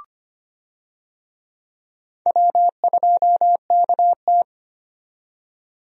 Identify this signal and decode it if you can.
Morse code, heard as a keyed tone, about 25 wpm, 700 Hz. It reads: W2KT
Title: USA Call Signs - 4 Characters - 25wpm